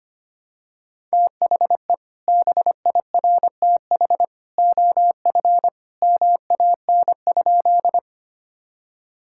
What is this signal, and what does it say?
Morse code, heard as a keyed tone, about 25 wpm, 700 Hz. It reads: THE BIRTH OF MAN?
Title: Sentences from Top 1000 Words - Review Part 5 - 25wpm